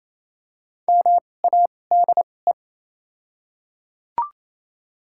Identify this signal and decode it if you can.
Morse code, heard as a keyed tone, about 28 wpm, spaced as for 20 wpm, 700 Hz. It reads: MADE
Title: 2x Character Spacing - Top 200 Words - 28wpm